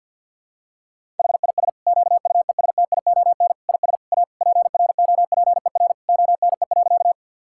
Morse code, heard as a keyed tone, about 48 wpm, 700 Hz. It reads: HIS QUESTION IS A PROPER ONE.
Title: Sentences from Top 1000 Words - Review Part 5 - 50wpm